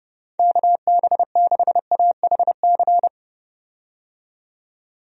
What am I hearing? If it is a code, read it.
Morse code, heard as a keyed tone, about 30 wpm, 700 Hz. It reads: KB6AHC
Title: USA Call Signs - 6 Characters - 30wpm